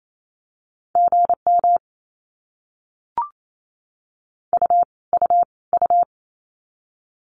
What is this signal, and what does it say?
Morse code, heard as a keyed tone, about 28 wpm, 700 Hz. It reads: GM U U U
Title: Abbreviations Repeated 3x - 28wpm